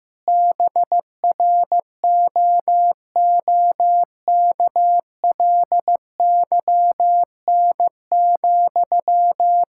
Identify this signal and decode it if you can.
Morse code, heard as a keyed tone, about 15 wpm, 700 Hz. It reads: BROOKLYN,
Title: Practice QSO - Exchange 1 - Beginning - 15wpm